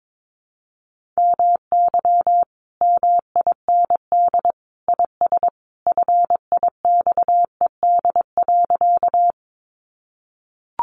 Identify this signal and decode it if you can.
Morse code, heard as a keyed tone, about 22 wpm, 700 Hz. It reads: MY MIND IS FIXED.
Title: Sentences from Top 1100 Words 22wpm